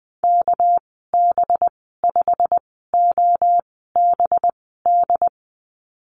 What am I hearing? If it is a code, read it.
Morse code, heard as a keyed tone, about 20 wpm, spaced as for 15 wpm, 700 Hz. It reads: KB5OBD
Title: 238 Lesson 38g - Letter q - Call Signs 20wpm